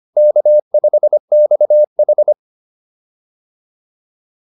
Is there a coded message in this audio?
Yes: Morse code from 0.2 to 2.3 s, heard as a keyed tone, about 25 wpm, 590 Hz. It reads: K5XH